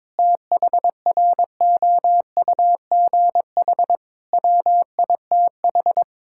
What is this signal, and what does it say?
Morse code, heard as a keyed tone, about 22 wpm, 700 Hz. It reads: THROUGH WITH
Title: Sentences from Top 1000 Words - Review Part 6 - 22wpm